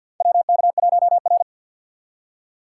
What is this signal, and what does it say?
Morse code, heard as a keyed tone, about 47 wpm, 700 Hz. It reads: WK1R
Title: USA Call Signs - 4 Characters - 2x Character Spacing - 50wpm